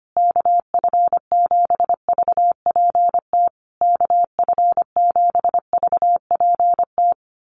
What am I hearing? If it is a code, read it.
Morse code, heard as a keyed tone, about 25 wpm, 700 Hz. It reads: KF7VPT KF7VPT
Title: USA Call Signs Repeated 3x 25wpm